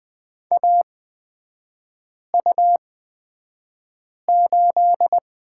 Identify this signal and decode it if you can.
Morse code, heard as a keyed tone, about 20 wpm, spaced as for 5 wpm, 700 Hz. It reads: AU8
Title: 236 Lesson 38e - Letter q - Three Characters 20wpm